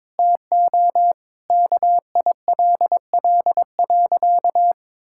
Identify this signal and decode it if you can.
Morse code, heard as a keyed tone, about 22 wpm, 700 Hz. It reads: TO KILL.
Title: Sentences from Top 1600 Words 22wpm